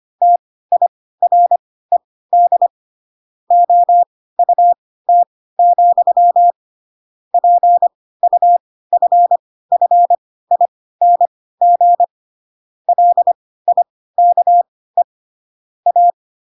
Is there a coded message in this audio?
Yes: Morse code from 0.2 to 16.1 s, heard as a keyed tone, about 25 wpm, spaced as for 16 wpm, 700 Hz. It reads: TIRED OUT, PUFFING LIKE A